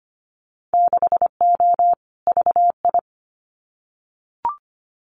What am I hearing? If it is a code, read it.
Morse code, heard as a keyed tone, about 25 wpm, 700 Hz. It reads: 6O VI
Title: Winter Field Day Exchanges Encore 25wpm